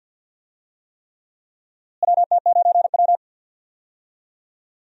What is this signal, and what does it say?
Morse code, heard as a keyed tone, about 46 wpm, 700 Hz. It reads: WT9W
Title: USA Call Signs - 4 Characters - Repeated 3x - 50wpm